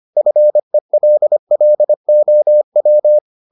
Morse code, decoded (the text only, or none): FELLOW